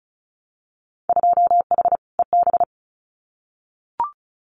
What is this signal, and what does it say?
Morse code, heard as a keyed tone, about 36 wpm, 700 Hz. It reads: 2H EB
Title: Winter Field Day Exchanges Encore 35wpm